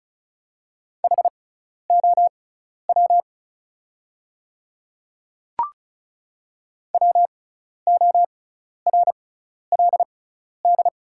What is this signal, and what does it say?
Morse code, heard as a keyed tone, about 34 wpm, spaced as for 12 wpm, 700 Hz. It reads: HOW WORLD
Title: 6x Character Spacing - Top 200 Words - 35wpm